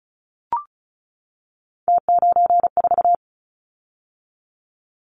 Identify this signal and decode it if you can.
Morse code, heard as a keyed tone, about 34 wpm, 700 Hz. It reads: T94
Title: Callsign Prefixes for Top 50 DXCC - 35wpm